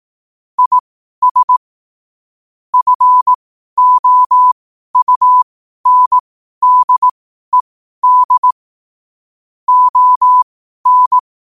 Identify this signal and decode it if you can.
Morse code, heard as a keyed tone, about 18 wpm, spaced as for 12 wpm, 1000 Hz. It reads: IS FOUNDED ON
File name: qotdfiles / 12wpm